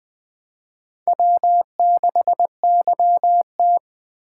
Morse code, heard as a keyed tone, about 24 wpm, 700 Hz. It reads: W6YT
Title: Speed Racing - USA 1x2 Call Signs 20wpm